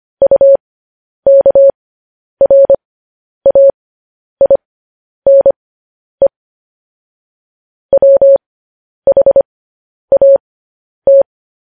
Morse code, heard as a keyed tone, about 25 wpm, spaced as for 10 wpm, 550 Hz. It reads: UKRAINE WHAT
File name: podcast / News headlines-1x-25x10